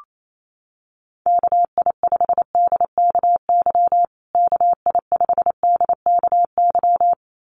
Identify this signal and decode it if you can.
Morse code, heard as a keyed tone, about 28 wpm, 700 Hz. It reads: KI5DKY KI5DKY
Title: USA Call Signs Repeated 2x 28wpm